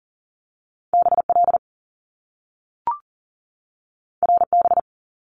Morse code, heard as a keyed tone, about 40 wpm, 700 Hz. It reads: BL RB
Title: Top Two Letter Combinations in Words 40wpm